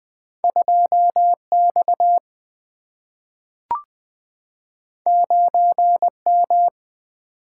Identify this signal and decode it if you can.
Morse code, heard as a keyed tone, about 20 wpm, 700 Hz. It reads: 2 X 9 M
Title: Single-Letter Single-Number 20wpm